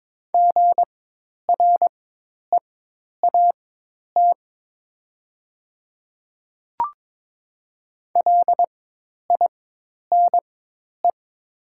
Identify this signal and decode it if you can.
Morse code, heard as a keyed tone, about 22 wpm, spaced as for 10 wpm, 700 Hz. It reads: GREAT LINE
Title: 4x Character Spacing - Top 200 Words - 22wpm